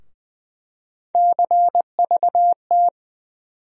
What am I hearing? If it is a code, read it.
Morse code, heard as a keyed tone, about 20 wpm, 700 Hz. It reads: CVT
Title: USA Call Sign Postfixes - 3 Letters 20wpm